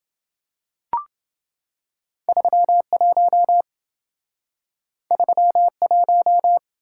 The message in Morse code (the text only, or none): 31 31